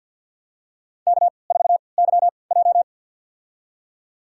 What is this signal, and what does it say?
Morse code, heard as a keyed tone, about 52 wpm, spaced as for 29 wpm, 700 Hz. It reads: K4YJ
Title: USA Call Signs - 4 Characters - 3x Character Spacing - 50wpm